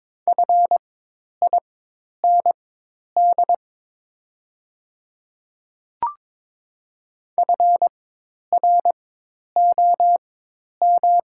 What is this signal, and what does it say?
Morse code, heard as a keyed tone, about 22 wpm, spaced as for 10 wpm, 700 Hz. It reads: FIND FROM
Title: 4x Character Spacing - Top 100 Words - 22wpm